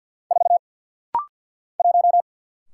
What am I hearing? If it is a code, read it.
Morse code, heard as a keyed tone, about 50 wpm, 700 Hz. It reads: 4 1
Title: Single Letter-Number - Rapid-Fire 50wpm